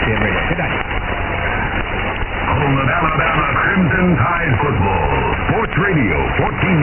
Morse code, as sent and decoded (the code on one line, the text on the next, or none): none